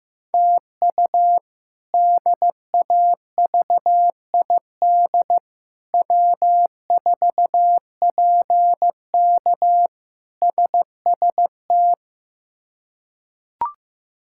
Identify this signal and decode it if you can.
Morse code, heard as a keyed tone, about 16 wpm, 700 Hz. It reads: TU DAVID W4PK SST
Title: K1USN SST QSO Practice - 15wpm